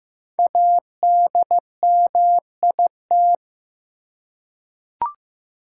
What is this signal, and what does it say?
Morse code, heard as a keyed tone, about 15 wpm, 700 Hz. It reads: ADMIT
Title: Five Letter Words - 15wpm